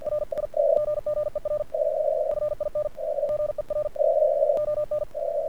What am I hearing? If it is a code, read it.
Morse code, heard as a keyed tone, about 48 wpm, spaced as for 37 wpm, 1200 Hz. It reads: RIGGER FINGER ON
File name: archived / CRSnet